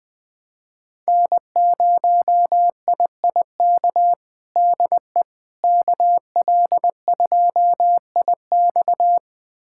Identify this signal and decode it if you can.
Morse code, heard as a keyed tone, about 20 wpm, 700 Hz. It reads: N0IIK DE KL2IX
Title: USA Callsign DE Callsign - 5 Characters - 20wpm